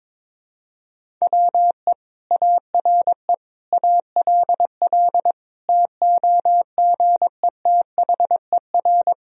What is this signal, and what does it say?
Morse code, heard as a keyed tone, about 22 wpm, 700 Hz. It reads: WE ARE ALL TOGETHER
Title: Sentences from Top 1000 Words - Review Part 2 - 22wpm